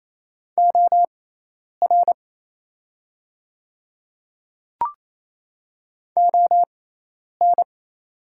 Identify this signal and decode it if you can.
Morse code, heard as a keyed tone, about 28 wpm, spaced as for 10 wpm, 700 Hz. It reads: OR ON
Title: 6x Character Spacing - Top 100 Words - 28wpm